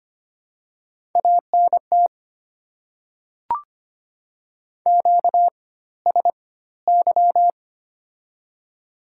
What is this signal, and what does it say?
Morse code, heard as a keyed tone, about 25 wpm, spaced as for 16 wpm, 700 Hz. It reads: ANT QSY
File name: Top 90 QSO Elements - 4x Character Spacing - 25wpm